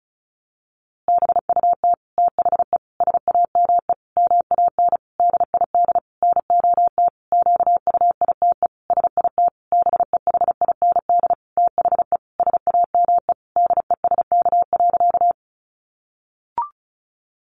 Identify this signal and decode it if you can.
Morse code, heard as a keyed tone, about 34 wpm, 700 Hz. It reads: BUT THE SAME MAN DID NOT QUITE SIT BEHIND THE SAME DESK.
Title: A Sound Of Thunder 35wpm